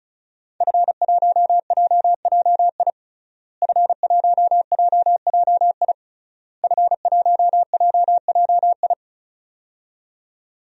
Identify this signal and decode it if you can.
Morse code, heard as a keyed tone, about 34 wpm, 700 Hz. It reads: F1JJI F1JJI F1JJI
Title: DXCC-100 Call Signs - 5 Characters - Repeated 3x - 35wpm